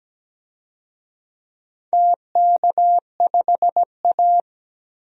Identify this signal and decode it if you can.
Morse code, heard as a keyed tone, about 17 wpm, 700 Hz. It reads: TK5A
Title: DXCC-100 Call Signs - 4 Characters - 17wpm